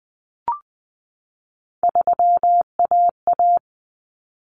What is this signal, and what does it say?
Morse code, heard as a keyed tone, about 20 wpm, 700 Hz. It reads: 3AA
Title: Common Cut Numbers - 3-Digit 20wpm